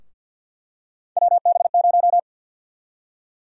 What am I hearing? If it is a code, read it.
Morse code, heard as a keyed tone, about 46 wpm, 700 Hz. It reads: WB0
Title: USA Call Sign Prefixes - 50wpm